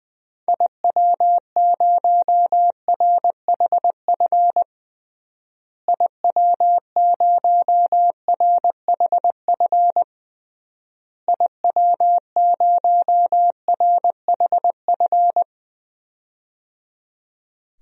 Morse code, heard as a keyed tone, about 21 wpm, 700 Hz. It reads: IW0RHF IW0RHF IW0RHF
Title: DXCC-100 Call Signs - 6 Characters - Repeated 3x - 20wpm